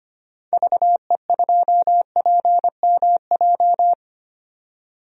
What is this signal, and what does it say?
Morse code, heard as a keyed tone, about 25 wpm, 700 Hz. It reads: VE2PMJ
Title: DXCC-100 Call Signs - 6 Characters - 25wpm